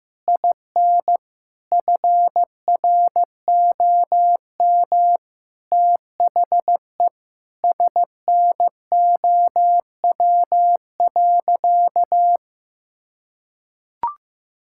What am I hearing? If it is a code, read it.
Morse code, heard as a keyed tone, about 15 wpm, 700 Hz. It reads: IN FROM THE SNOW.
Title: Sentences from Top 1000 Words - Review Part 5 - 15wpm